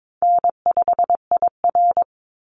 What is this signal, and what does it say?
Morse code, heard as a keyed tone, about 22 wpm, 700 Hz. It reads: N5IR
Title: Easy Call Signs - 22wpm